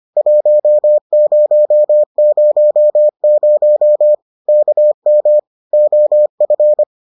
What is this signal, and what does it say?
Morse code, heard as a keyed tone, about 25 wpm, 600 Hz. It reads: 1000 KM OF